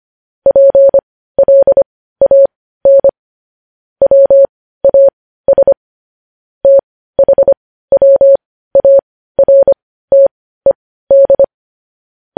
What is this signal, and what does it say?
Morse code, heard as a keyed tone, about 25 wpm, spaced as for 15 wpm, 550 Hz. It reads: PLAN WAS THWARTED
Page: news-headlines-in-morse-code-at-15-wpm